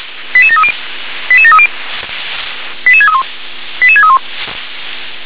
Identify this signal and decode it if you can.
Five-tone selective call (ZVEI), heard as quick runs of five tones, tones of about 70 ms. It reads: 88730 88730 88631 88631